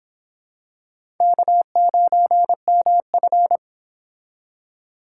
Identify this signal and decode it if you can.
Morse code, heard as a keyed tone, about 26 wpm, 700 Hz. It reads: K9MF